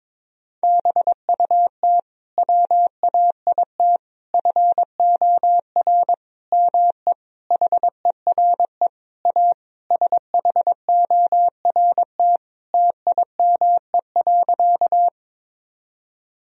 Morse code, heard as a keyed tone, about 22 wpm, 700 Hz. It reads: BUT WAIT FOR ME HERE A SHORT TIME.
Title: Sentences from Top 1000 Words - Review Part 2 - 22wpm